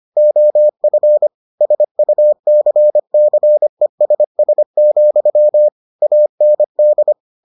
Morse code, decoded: OF SUCCESS, AND

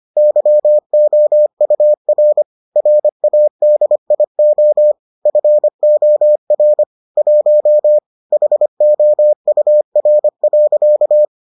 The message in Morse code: YOUR RADIO FOR 1 HOUR.

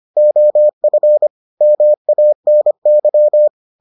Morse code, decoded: OF MANY

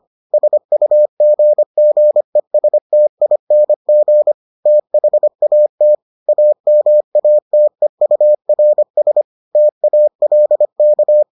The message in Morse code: SUGGESTING THAT AMATEURS TALK

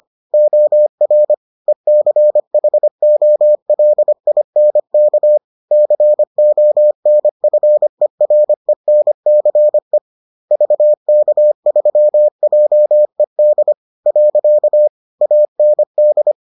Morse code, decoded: OR ECHOLINK CONFERENCE VK3JED. AND

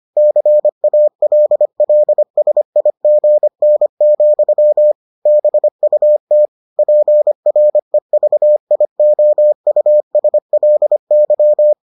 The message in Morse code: CALLSIGN, BUT PREVIOUSLY

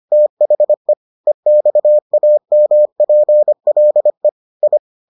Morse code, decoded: THE EXAMPLE I

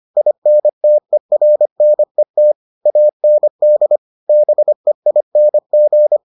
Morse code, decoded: INTERNET AND BEING

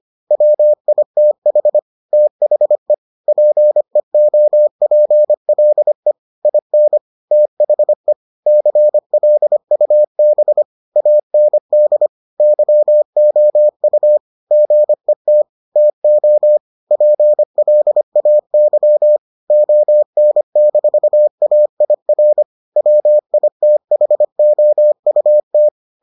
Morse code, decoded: WITH THE PEOPLE IN THE CLUB AND YOU GET TO PLAY ON-AIR WITHOUT

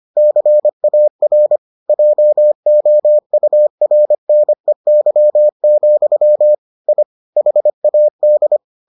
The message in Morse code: CAR JOURNEY, I HAD

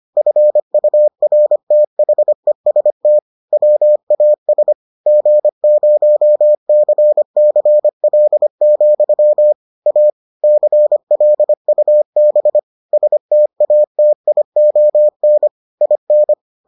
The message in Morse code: FURTHEST WAS G0CCL, A CLUB STATION IN

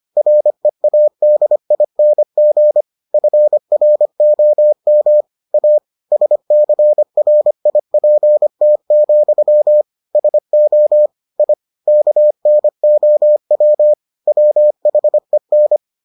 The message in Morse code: READING FROM A SCRIPT, SO I KNOW WHEN

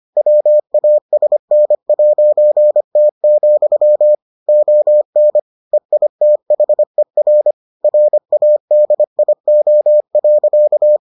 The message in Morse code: WASN'T, ON EITHER RADIO.

